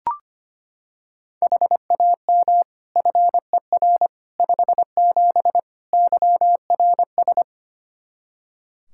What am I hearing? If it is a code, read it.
Morse code, heard as a keyed tone, about 25 wpm, 700 Hz. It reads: HAM FER 57 YRS
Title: K1USN SST QSO Practice 25wpm